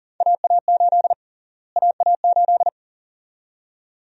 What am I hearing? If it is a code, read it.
Morse code, heard as a keyed tone, about 40 wpm, 700 Hz. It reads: AA8 AA8
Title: USA Call Sign Prefixes - Repeated 2x - 40wpm